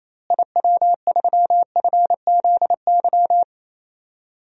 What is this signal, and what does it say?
Morse code, heard as a keyed tone, about 28 wpm, 700 Hz. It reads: IW3FZY
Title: DXCC-100 Call Signs - 6 Characters - Repeated 3x - 28wpm